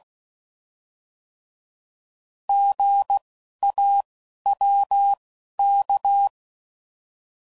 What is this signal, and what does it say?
Morse code, heard as a keyed tone, about 16 wpm, spaced as for 11 wpm, 800 Hz. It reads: GAWK
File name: Prefixes / By First Letter